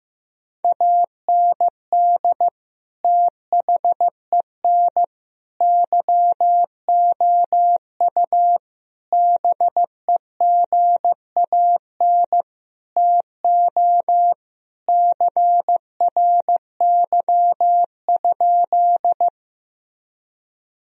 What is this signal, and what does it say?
Morse code, heard as a keyed tone, about 15 wpm, 700 Hz. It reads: AND THEN YOU BEGAN TO CRY?
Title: Sentences from Top 1000 Words - Review Part 2 - 15wpm